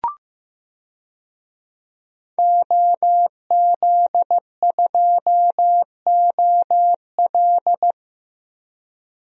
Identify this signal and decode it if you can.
Morse code, heard as a keyed tone, about 15 wpm, 700 Hz. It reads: OZ2OL
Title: DXCC-100 Call Signs - 5 Characters - 15wpm